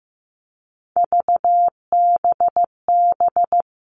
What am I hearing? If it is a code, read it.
Morse code, heard as a keyed tone, about 15 wpm, 700 Hz. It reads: VBB
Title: Three Challenging Letter-Number Combinations 15wpm